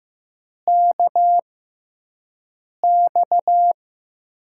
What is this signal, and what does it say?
Morse code, heard as a keyed tone, about 15 wpm, spaced as for 5 wpm, 700 Hz. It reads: KX
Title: USA Call Signs - 6x Character Spacing - 15wpm